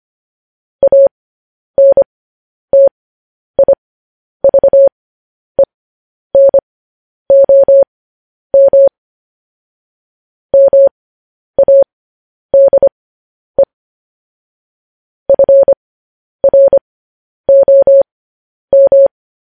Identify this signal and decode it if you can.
Morse code, heard as a keyed tone, about 25 wpm, spaced as for 10 wpm, 550 Hz. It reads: ANTIVENOM MADE FROM
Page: news-headlines-in-morse-code-at-10-wpm